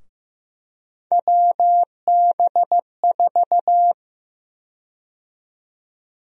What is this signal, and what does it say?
Morse code, heard as a keyed tone, about 16 wpm, 700 Hz. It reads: WB4
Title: USA Call Sign Prefixes - Repeated 2x - 15wpm